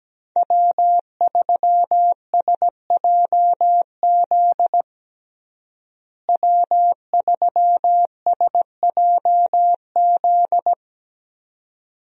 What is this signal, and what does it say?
Morse code, heard as a keyed tone, about 17 wpm, 700 Hz. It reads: W3SJZ W3SJZ
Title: USA Call Signs - 5 Characters - Repeated 2x - 17wpm